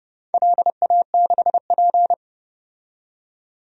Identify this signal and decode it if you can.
Morse code, heard as a keyed tone, about 31 wpm, 700 Hz. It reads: LA6P
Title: DXCC-100 Call Signs - 4 Characters - 30wpm